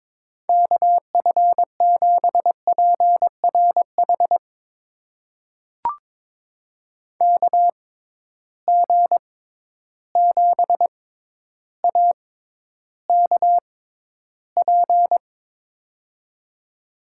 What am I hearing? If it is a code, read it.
Morse code, heard as a keyed tone, about 22 wpm, spaced as for 11 wpm, 700 Hz. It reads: KF7PRH KG7AKP
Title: USA Call Signs - 6 Characters - 6x Character Spacing - 22wpm